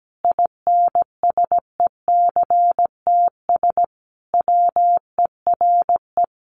Morse code, heard as a keyed tone, about 17 wpm, 700 Hz. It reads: INSECTS WERE
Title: A Sound Of Thunder 17wpm